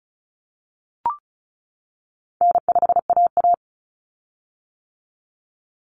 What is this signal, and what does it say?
Morse code, heard as a keyed tone, about 34 wpm, 700 Hz. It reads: N5AA
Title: Common Cut Numbers - 4-Digit 35wpm